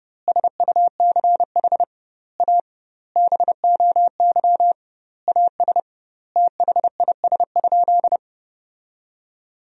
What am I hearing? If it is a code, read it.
Morse code, heard as a keyed tone, about 30 wpm, 700 Hz. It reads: SUCH A BOY AS THIS?
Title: 2x Word-Spacing - Sentences from Top 400 Words - 30wpm